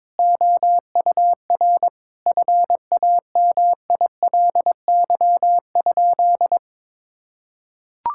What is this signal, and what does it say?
Morse code, heard as a keyed tone, about 22 wpm, 700 Hz. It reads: OUR FAMILY?
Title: Sentences from Top 1000 Words - Review Part 6 - 22wpm